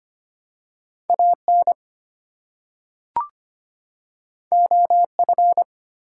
Morse code, heard as a keyed tone, about 25 wpm, 700 Hz. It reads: AN OF
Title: Two Letter Words - 25wpm